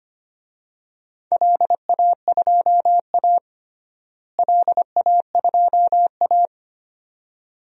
Morse code, heard as a keyed tone, about 25 wpm, 700 Hz. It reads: LA2A LA2A
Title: DXCC-100 Call Signs - 4 Characters - Repeated 2x - 25wpm